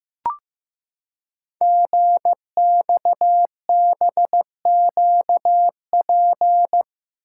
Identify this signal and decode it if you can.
Morse code, heard as a keyed tone, about 15 wpm, 700 Hz. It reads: GXBQP
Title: Code Groups - 5 Letters - 15wpm